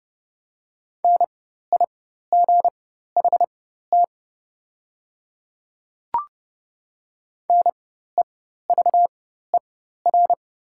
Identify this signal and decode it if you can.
Morse code, heard as a keyed tone, about 30 wpm, spaced as for 14 wpm, 700 Hz. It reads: NIGHT NEVER
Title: 4x Character Spacing - Top 300 Words - 30wpm